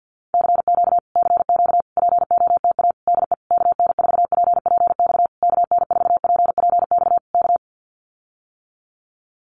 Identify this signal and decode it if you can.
Morse code, heard as a keyed tone, about 48 wpm, 700 Hz. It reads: CQ CQ POTA DE KN4PPX KN4PPX K
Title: Common QSO Phrases Encore 50wpm